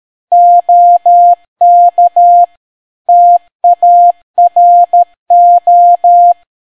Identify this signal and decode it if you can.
Morse code, heard as a keyed tone, about 13 wpm, 700 Hz. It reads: OK TARO